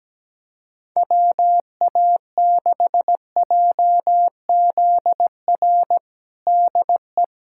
Morse code, heard as a keyed tone, about 17 wpm, 700 Hz. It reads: WA6JZR DE
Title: USA Callsign DE Callsign - 6 Characters - 17wpm